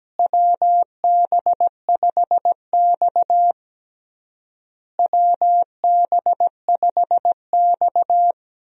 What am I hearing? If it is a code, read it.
Morse code, heard as a keyed tone, about 17 wpm, 700 Hz. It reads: WB5X WB5X
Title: USA Call Signs - 4 Characters - Repeated 3x - 17wpm